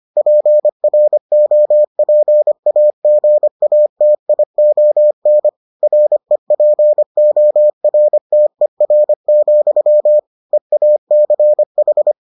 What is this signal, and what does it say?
Morse code, heard as a keyed tone, about 25 wpm, 600 Hz. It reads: PROPAGATION REPORTER, EACH